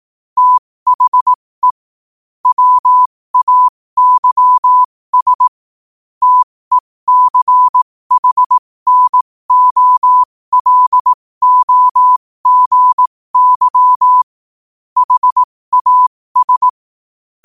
Morse code, heard as a keyed tone, about 18 wpm, spaced as for 15 wpm, 1000 Hz. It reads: THE WAYS TECHNOLOGY HAS
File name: qotdfiles / 15wpm